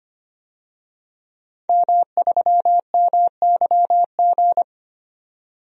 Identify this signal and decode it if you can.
Morse code, heard as a keyed tone, about 25 wpm, 700 Hz. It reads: M3MYG
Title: DXCC-100 Call Signs - 5 Characters - 25wpm